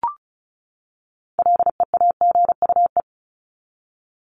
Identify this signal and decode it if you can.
Morse code, heard as a keyed tone, about 34 wpm, 700 Hz. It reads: LEAGUE
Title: Top 1751-2000 Words 35wpm